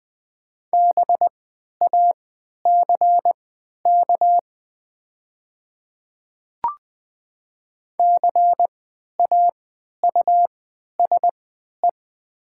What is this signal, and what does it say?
Morse code, heard as a keyed tone, about 20 wpm, spaced as for 11 wpm, 700 Hz. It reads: BACK CAUSE
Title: 3x Character Spacing - Top 200 Words - 20wpm